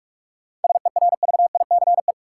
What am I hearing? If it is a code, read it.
Morse code, heard as a keyed tone, about 45 wpm, 700 Hz. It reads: SERVICE
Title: Sentences from Top 700 Words - New Word Drill - 45wpm